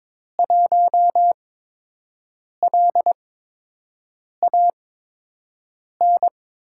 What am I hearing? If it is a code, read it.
Morse code, heard as a keyed tone, about 22 wpm, spaced as for 6 wpm, 700 Hz. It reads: 1LAN
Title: USA Call Signs - 5 Characters - 8x Character Spacing - 22wpm